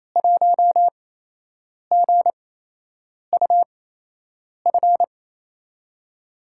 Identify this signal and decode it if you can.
Morse code, heard as a keyed tone, about 28 wpm, spaced as for 8 wpm, 700 Hz. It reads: 1GUF
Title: USA Call Signs - 8x Character Spacing - 28wpm